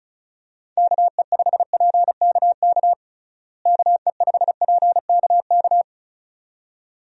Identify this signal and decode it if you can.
Morse code, heard as a keyed tone, about 36 wpm, 700 Hz. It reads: KE5PKK KE5PKK
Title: USA Call Signs - 6 Characters - Repeated 2x - 35wpm